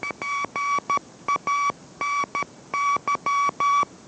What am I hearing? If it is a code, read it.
Morse code, heard as a keyed tone, about 13 wpm, 1140 Hz. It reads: GANY